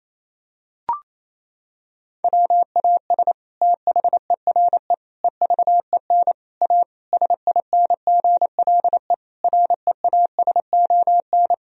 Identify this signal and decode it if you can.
Morse code, heard as a keyed tone, about 27 wpm, 700 Hz. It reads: WAS THERE EVEN A SINGLE REASON
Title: Sentences from Top 1000 Words - Review Part 4 - 28wpm